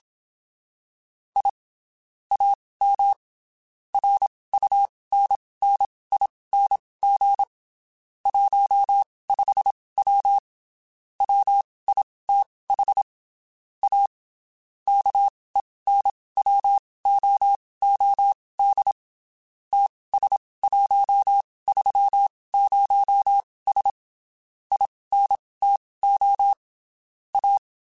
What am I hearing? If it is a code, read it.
Morse code, heard as a keyed tone, about 27 wpm, spaced as for 19 wpm, 800 Hz. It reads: I AM RUNNING 15W WITH A KENWOOD TS130S INTO A
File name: (spaced) / QSOs Spaced-mp3s / WD0BGP 1 @25 sp